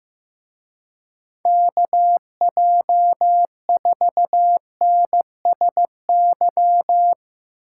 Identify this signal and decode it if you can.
Morse code, heard as a keyed tone, about 16 wpm, 700 Hz. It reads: KJ4NSY
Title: Speed Racing - USA Call Signs 15wpm